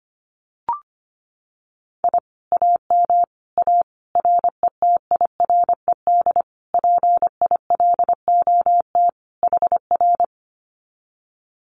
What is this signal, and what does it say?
Morse code, heard as a keyed tone, about 25 wpm, 700 Hz. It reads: I AM A RETIRED PILOT HR
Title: Common QSO Phrases Encore 25wpm